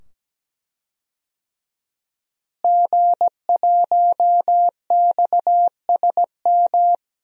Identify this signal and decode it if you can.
Morse code, heard as a keyed tone, about 17 wpm, 700 Hz. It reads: G1XSM
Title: DXCC-100 Call Signs - 5 Characters - Repeated 2x - 17wpm